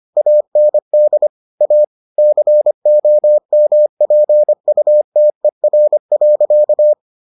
Morse code, heard as a keyed tone, about 25 wpm, 600 Hz. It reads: AND A COMPUTER.